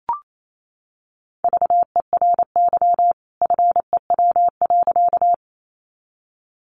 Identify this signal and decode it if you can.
Morse code, heard as a keyed tone, about 28 wpm, 700 Hz. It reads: VERY FEW.
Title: A Sound Of Thunder 28wpm